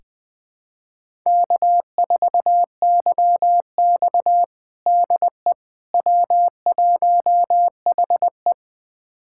Morse code, heard as a keyed tone, about 21 wpm, 700 Hz. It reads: K4YX DE W1HE
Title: USA Callsign DE Callsign - 4 Characters - 20wpm